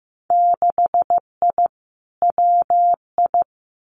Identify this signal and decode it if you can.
Morse code, heard as a keyed tone, about 15 wpm, 700 Hz. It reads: 6I WI
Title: Winter Field Day Exchanges Encore 15wpm